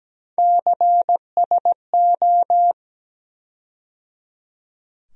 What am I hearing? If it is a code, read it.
Morse code, heard as a keyed tone, about 17 wpm, 700 Hz. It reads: CSO
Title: USA Call Sign Postfixes - 3 Letters 17wpm